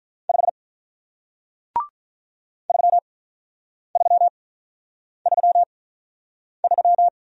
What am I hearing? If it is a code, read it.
Morse code, heard as a keyed tone, about 45 wpm, spaced as for 9 wpm, 700 Hz. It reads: 5 3333